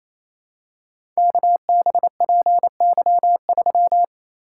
Morse code, heard as a keyed tone, about 28 wpm, 700 Hz. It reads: KBPY3
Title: Code Groups - 5 Letter:Numbers - 28wpm